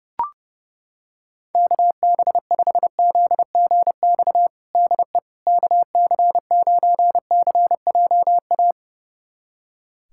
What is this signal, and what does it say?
Morse code, heard as a keyed tone, about 32 wpm, 700 Hz. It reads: KB5ZGX DE KC9CJA
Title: USA Callsign DE Callsign - 6 Characters - 30wpm